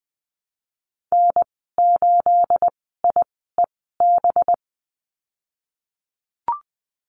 Morse code, heard as a keyed tone, about 20 wpm, spaced as for 14 wpm, 700 Hz. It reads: N8IEB
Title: 245 Lesson 39g - Letter z - Call Signs 20wpm